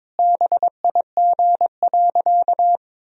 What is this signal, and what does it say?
Morse code, heard as a keyed tone, about 22 wpm, 700 Hz. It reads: BIG.
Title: Sentences from Top 1000 Words - Review Part 1 - 22wpm